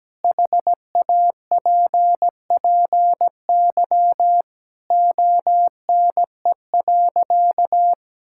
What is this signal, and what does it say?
Morse code, heard as a keyed tone, about 17 wpm, 700 Hz. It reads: HAPPY ONE.
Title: Sentences from Top 1000 Words - Review Part 5 - 17wpm